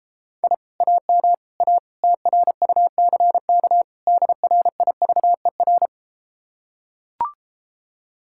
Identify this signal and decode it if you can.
Morse code, heard as a keyed tone, about 33 wpm, 700 Hz. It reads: I AM A TRUCK DRIVER